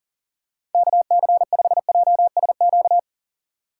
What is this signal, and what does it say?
Morse code, heard as a keyed tone, about 40 wpm, 700 Hz. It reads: KC5JSQ
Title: USA Call Signs - 3x Character Spacing - 40wpm